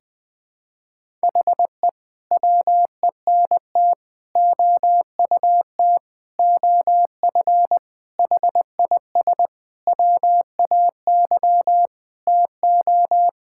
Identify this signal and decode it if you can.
Morse code, heard as a keyed tone, about 20 wpm, 700 Hz. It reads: HE WENT OUT OF HIS WAY TO
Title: Sentences with 6-Word Phrases - Spoken Beginning - 20wpm